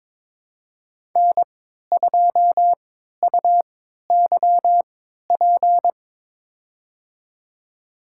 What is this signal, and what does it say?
Morse code, heard as a keyed tone, about 22 wpm, spaced as for 13 wpm, 700 Hz. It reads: N2UYP
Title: USA Call Signs - 3x Character Spacing - 22wpm